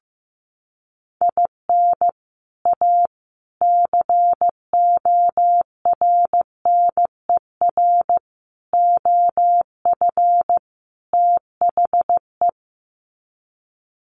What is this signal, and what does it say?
Morse code, heard as a keyed tone, about 15 wpm, 700 Hz. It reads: IN A CORNER OF THE
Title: Sets of 5 Words Encore2 15wpm